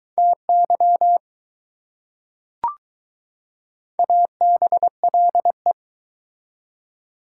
Morse code, heard as a keyed tone, about 22 wpm, 700 Hz. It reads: TY ABLE